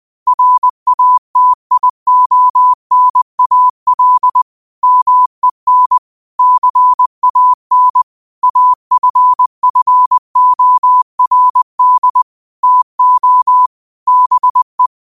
Morse code, heard as a keyed tone, about 20 wpm, 1000 Hz. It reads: RATIONAL MEN CAN AFFORD TO BE
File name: qotdfiles / 20wpm